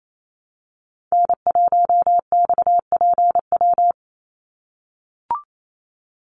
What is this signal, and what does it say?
Morse code, heard as a keyed tone, about 28 wpm, 700 Hz. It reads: N1XPW
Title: USA Call Signs 28wpm